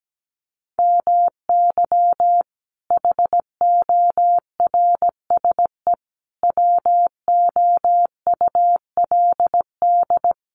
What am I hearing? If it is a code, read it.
Morse code, heard as a keyed tone, about 17 wpm, 700 Hz. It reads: MY HORSE WOULD
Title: Sentences from Top 1400 Words 17wpm